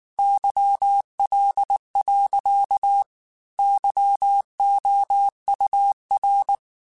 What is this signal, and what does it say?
Morse code, heard as a keyed tone, about 19 wpm, 800 Hz. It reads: YL. YOUR
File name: QSOs / QSOs-mp3s / WA7QOX 2@18